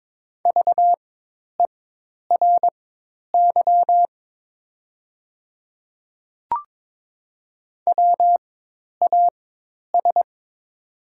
Morse code, heard as a keyed tone, about 21 wpm, spaced as for 10 wpm, 700 Hz. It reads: VERY WAS
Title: 4x Character Spacing - Top 200 Words - 22wpm